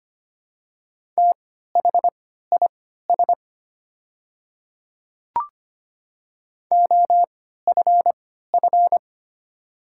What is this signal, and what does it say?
Morse code, heard as a keyed tone, about 25 wpm, spaced as for 14 wpm, 700 Hz. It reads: THIS OFF
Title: 3x Character Spacing - Top 200 Words - 25wpm